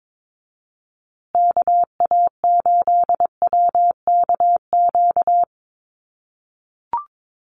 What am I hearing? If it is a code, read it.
Morse code, heard as a keyed tone, about 22 wpm, 700 Hz. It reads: KA8WKQ
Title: USA Call Signs Repeated 2x 22wpm